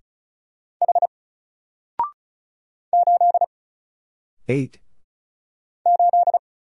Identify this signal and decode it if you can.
Morse code, heard as a keyed tone, about 36 wpm, 700 Hz. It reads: H 8 8
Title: Single Letter-Number Alt 35wpm